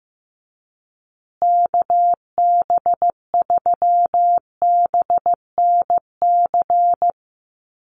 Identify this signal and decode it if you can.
Morse code, heard as a keyed tone, about 15 wpm, 700 Hz. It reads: KB3BNC
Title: USA Call Signs Repeated 3x 15wpm